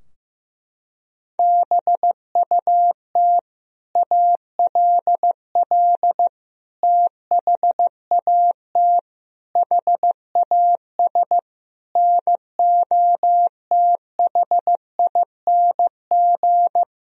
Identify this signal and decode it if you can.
Morse code, heard as a keyed tone, about 16 wpm, 700 Hz. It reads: BUT ALL THAT HAS NOTHING
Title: Sentences from Top 1000 Words - Review Part 3 - 15wpm